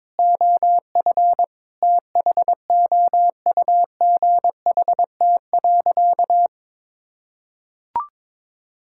Morse code, heard as a keyed tone, about 22 wpm, 700 Hz. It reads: OF THOUGHT.
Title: Sentences from Top 1000 Words - Review Part 5 - 22wpm